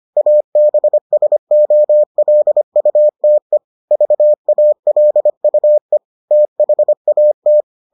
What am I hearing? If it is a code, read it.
Morse code, heard as a keyed tone, about 25 wpm, 600 Hz. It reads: ABSOLUTE VALUE THAT